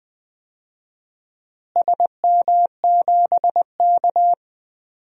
Morse code, heard as a keyed tone, about 20 wpm, 700 Hz. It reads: SM7K
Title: DXCC-100 Call Signs - 4 Characters - Repeated 3x - 20wpm